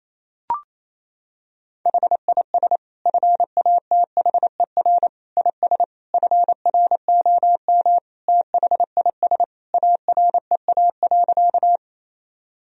Morse code, heard as a keyed tone, about 27 wpm, 700 Hz. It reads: HIS FATHER IS FROM THIS AREA.
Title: Sentences from Top 1000 Words - Review Part 2 - 28wpm